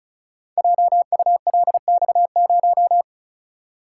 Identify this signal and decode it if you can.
Morse code, heard as a keyed tone, about 36 wpm, 690 Hz. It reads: JULX0
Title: Code Groups - 5 Letter:Numbers - 35wpm